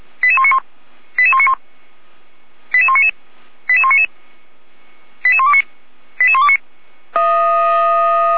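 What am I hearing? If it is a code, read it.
Five-tone selective call (ZVEI), heard as quick runs of five tones, tones of about 70 ms. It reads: 88191 88191 88199 88199 88129 88129
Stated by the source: ZVEI five-tone selective call received on an SDR.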